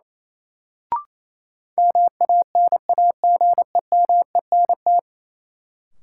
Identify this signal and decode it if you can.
Morse code, heard as a keyed tone, about 28 wpm, 700 Hz. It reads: MANAGEMENT
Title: Sentences from Top 1000 Words - Review Part 4 - New Word Drill - 28wpm